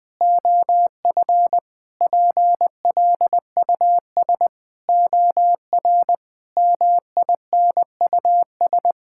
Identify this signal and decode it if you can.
Morse code, heard as a keyed tone, about 20 wpm, 700 Hz. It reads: OF PLUS OR MINUS
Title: Sentences with 8-Word Phrases - 20wpm